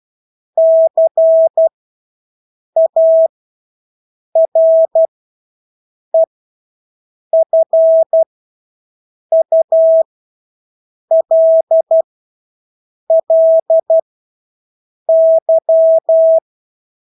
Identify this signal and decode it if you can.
Morse code, heard as a keyed tone, about 12 wpm, spaced as for 6 wpm, 650 Hz. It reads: CAREFULLY
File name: Engelske ord 12-6 wpm